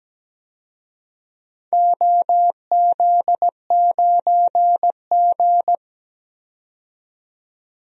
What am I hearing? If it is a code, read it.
Morse code, heard as a keyed tone, about 17 wpm, 700 Hz. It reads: OZ9G
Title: DXCC-100 Call Signs - 4 Characters - 17wpm